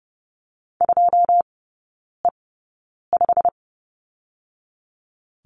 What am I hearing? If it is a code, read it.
Morse code, heard as a keyed tone, about 30 wpm, spaced as for 9 wpm, 700 Hz. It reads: 2E5
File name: Lesson 15e - Number 5 - Three Characters 30wpm